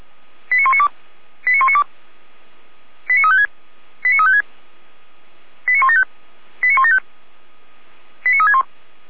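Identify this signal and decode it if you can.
Five-tone selective call (ZVEI), heard as quick runs of five tones, tones of about 70 ms. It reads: 89192 89192 89356 89356 89175 89175 89361
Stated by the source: ZVEI five-tone call received with an SDR.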